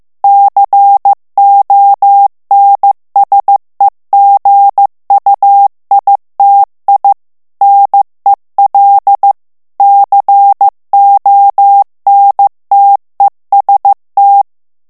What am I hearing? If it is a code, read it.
Morse code, heard as a keyed tone, about 15 wpm, 800 Hz. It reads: CONSEGUITI NEL CONTEST
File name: morse mp3 / ITALIEN texte clair 2 vitesse 14 mots minute